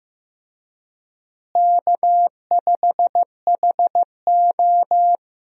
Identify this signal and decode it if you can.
Morse code, heard as a keyed tone, about 15 wpm, 700 Hz. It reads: K5HO
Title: USA Call Signs - 4 Characters - Repeated 3x - 15wpm